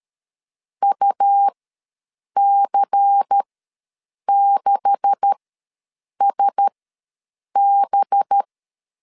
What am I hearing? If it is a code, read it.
Morse code, heard as a keyed tone, about 13 wpm, spaced as for 7 wpm, 800 Hz. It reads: UC6SB
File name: LEZIONI / 9 286 ws 12wpm 3351